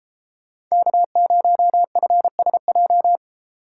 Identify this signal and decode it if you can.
Morse code, heard as a keyed tone, about 36 wpm, 700 Hz. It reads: K0FSJ